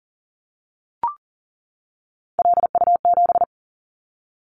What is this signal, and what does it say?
Morse code, heard as a keyed tone, about 40 wpm, 700 Hz. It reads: LU7
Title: Callsign Prefixes for Top 100 DXCC - 40wpm